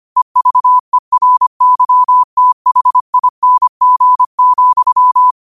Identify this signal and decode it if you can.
Morse code, heard as a keyed tone, about 25 wpm, 1000 Hz. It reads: EVERYTHING,